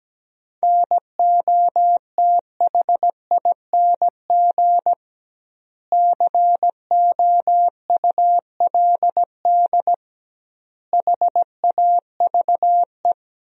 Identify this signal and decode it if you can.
Morse code, heard as a keyed tone, about 17 wpm, 700 Hz. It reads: NOTHING COULD HAVE
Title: 2x Word-Spacing - Sentences from Top 500 Words - 17wpm